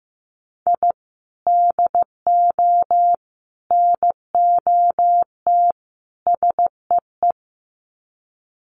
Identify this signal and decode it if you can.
Morse code, heard as a keyed tone, about 15 wpm, 700 Hz. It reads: I DO NOT SEE
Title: Sets of 4 Words 15wpm